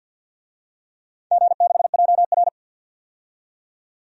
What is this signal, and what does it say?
Morse code, heard as a keyed tone, about 53 wpm, 700 Hz. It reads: G6JR
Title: DXCC-100 Call Signs - 4 Characters - 50wpm